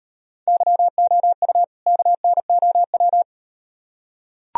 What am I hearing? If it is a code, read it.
Morse code, heard as a keyed tone, about 38 wpm, 700 Hz. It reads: YOU KNOW